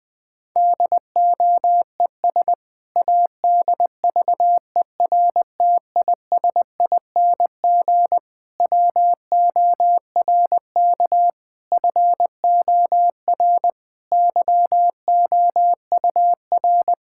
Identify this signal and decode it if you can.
Morse code, heard as a keyed tone, about 20 wpm, 700 Hz. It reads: DOES ADVERTISING WORK FOR YOUR
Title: Sentences from Top 1000 Words - Review Part 6 - 20wpm